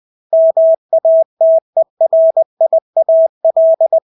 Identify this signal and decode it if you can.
Morse code, heard as a keyed tone, about 20 wpm, 650 Hz. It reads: MATERIAL
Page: Engelske ord 20 wpm